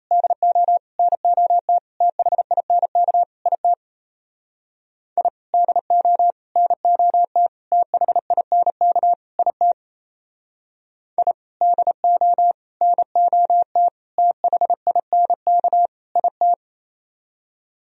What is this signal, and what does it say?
Morse code, heard as a keyed tone, about 36 wpm, 700 Hz. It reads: DO NOT THINK IT I DO NOT THINK IT I DO NOT THINK IT